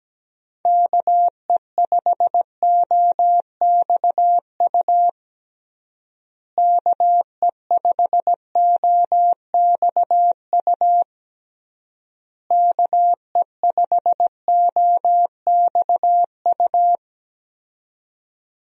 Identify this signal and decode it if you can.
Morse code, heard as a keyed tone, about 17 wpm, 700 Hz. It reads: KE5OXU KE5OXU KE5OXU
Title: USA Call Signs - 6 Characters - Repeated 3x - 17wpm